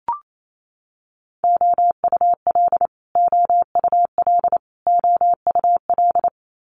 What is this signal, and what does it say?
Morse code, heard as a keyed tone, about 28 wpm, 700 Hz. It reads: OUL OUL OUL
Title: Multi-letter Phonemes Repeated 3x 28wpm